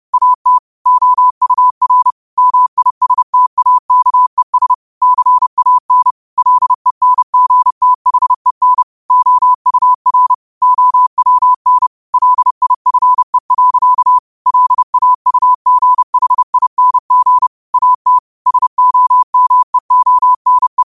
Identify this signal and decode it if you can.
Morse code, heard as a keyed tone, about 30 wpm, 1000 Hz. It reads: AT OUR MISTAKES CAN LENGTHEN OUR OWN LIFE. LAUGHING AT SOMEONE